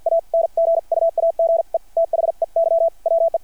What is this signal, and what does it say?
Morse code, heard as a keyed tone, about 53 wpm, spaced as for 43 wpm, 660 Hz. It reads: ANGUAGE THEY WE
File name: archived / CRSnet